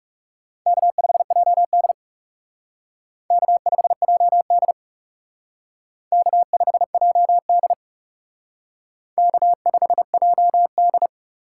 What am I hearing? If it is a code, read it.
Morse code, heard as a keyed tone, about 36 wpm, 700 Hz. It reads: K5JD K5JD K5JD K5JD
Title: Speed Racing - USA 1x2 Call Signs 30wpm